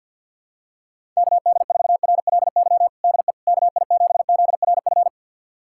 Morse code, heard as a keyed tone, about 47 wpm, 700 Hz. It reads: KD4RLY DE KI7ZRF
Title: USA Callsign DE Callsign 50wpm